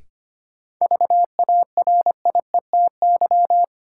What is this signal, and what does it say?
Morse code, heard as a keyed tone, about 25 wpm, 700 Hz. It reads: VARIETY
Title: Sentences from Top 1000 Words - Review Part 4 - New Word Drill - 25wpm